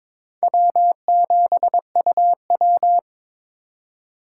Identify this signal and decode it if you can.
Morse code, heard as a keyed tone, about 23 wpm, 700 Hz. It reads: W7UW
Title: USA Call Signs - 4 Characters - 2x Character Spacing - 22wpm